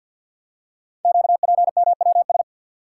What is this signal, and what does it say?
Morse code, heard as a keyed tone, about 51 wpm, 700 Hz. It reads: QPMWS
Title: Code Groups - 5 Letters - 50wpm